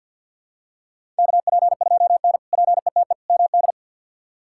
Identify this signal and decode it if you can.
Morse code, heard as a keyed tone, about 50 wpm, 700 Hz. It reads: KP2N PETE MD
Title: K1USN SST QSO Practice - 50wpm